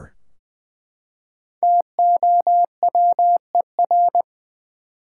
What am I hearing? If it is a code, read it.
Morse code, heard as a keyed tone, about 20 wpm, 700 Hz. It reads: TOWER
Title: Five Letter Words - 20wpm